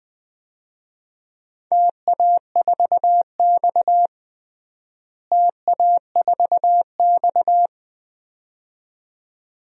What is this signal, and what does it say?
Morse code, heard as a keyed tone, about 21 wpm, 700 Hz. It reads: TA4X TA4X
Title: DXCC-100 Call Signs - 4 Characters - Repeated 2x - 20wpm